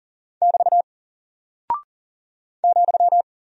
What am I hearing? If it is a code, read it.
Morse code, heard as a keyed tone, about 39 wpm, 700 Hz. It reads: <BT> ,
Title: Single Character (Letter:Number:Punctuation:Prosign) Alt 40wpm